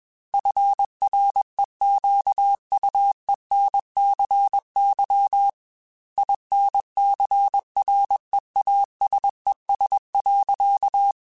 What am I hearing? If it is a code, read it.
Morse code, heard as a keyed tone, about 21 wpm, 800 Hz. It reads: FREQUENCY INCREASES.
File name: Technical Subjects / Technical Subjects-mp3s / XC @20